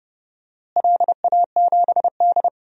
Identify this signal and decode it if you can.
Morse code, heard as a keyed tone, about 30 wpm, 700 Hz. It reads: LA7D
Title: DXCC-100 Call Signs - 4 Characters - Repeated 2x - 30wpm